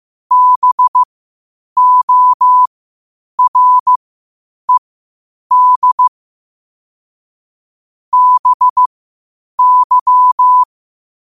Morse code, heard as a keyed tone, about 15 wpm, spaced as for 8 wpm, 1000 Hz. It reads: BORED BY
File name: qotdfiles / 8wpm